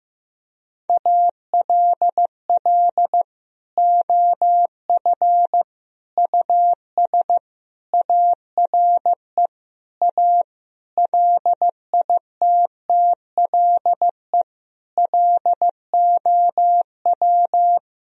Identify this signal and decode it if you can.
Morse code, heard as a keyed tone, about 15 wpm, 700 Hz. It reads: ALL OF US ARE A LITTLE LOW
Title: Sentences from Top 1000 Words - Review Part 3 - 15wpm